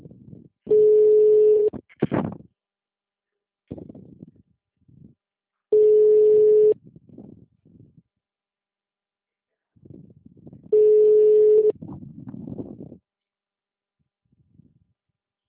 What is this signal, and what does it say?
Ringback tone (Europe, 425 Hz).